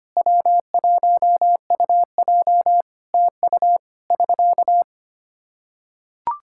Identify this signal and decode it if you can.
Morse code, heard as a keyed tone, about 25 wpm, 700 Hz. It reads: W1UJ TU <SK>
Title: K1USN SST QSO Practice 25wpm